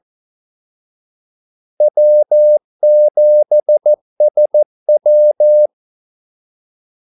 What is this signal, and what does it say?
Morse code, heard as a keyed tone, about 14 wpm, 600 Hz. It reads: W7SW
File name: Callsigns-14wpm